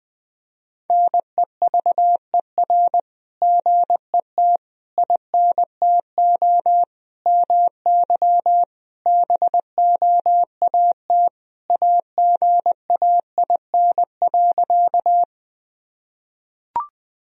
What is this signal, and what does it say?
Morse code, heard as a keyed tone, about 20 wpm, 700 Hz. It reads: NEVER GET INTO MY BOAT AGAIN.
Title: Sentences from Top 1000 Words - Review Part 3 - 20wpm